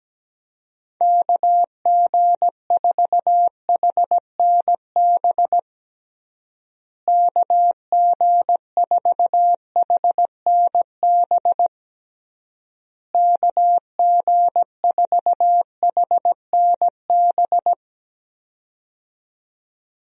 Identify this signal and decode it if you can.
Morse code, heard as a keyed tone, about 17 wpm, 700 Hz. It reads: KG4HNB KG4HNB KG4HNB
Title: USA Call Signs - 6 Characters - Repeated 3x - 17wpm